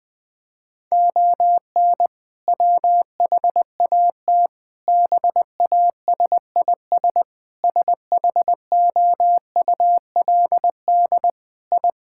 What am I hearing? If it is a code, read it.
Morse code, heard as a keyed tone, about 21 wpm, 700 Hz. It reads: ON WHAT BASIS SHOULD I
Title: Sentences from Top 1000 Words - Review Part 5 - 20wpm